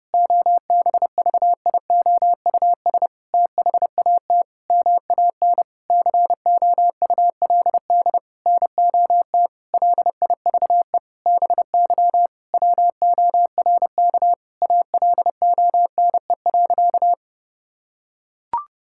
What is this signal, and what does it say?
Morse code, heard as a keyed tone, about 30 wpm, 700 Hz. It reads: OBVIOUS THAT MAN COULD NOT LIVE BY WORK ALONE.
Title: Sentences from Top 1600 Words 30wpm